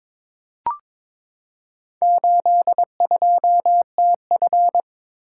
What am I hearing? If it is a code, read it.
Morse code, heard as a keyed tone, about 21 wpm, 700 Hz. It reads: 82TF
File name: practice / 22wpm / Four Letter-Number Combinations - 22wpm